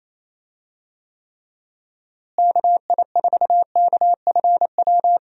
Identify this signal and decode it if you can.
Morse code, heard as a keyed tone, about 28 wpm, 700 Hz. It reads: KI4KFW
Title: USA Call Signs - 6 Characters - Repeated 3x - 28wpm